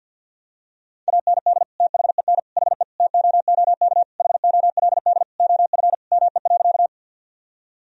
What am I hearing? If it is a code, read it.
Morse code, heard as a keyed tone, about 52 wpm, 700 Hz. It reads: AND THEN HE TOOK HOLD OF ME.
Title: Sentences from Top 1000 Words - Review Part 2 - 50wpm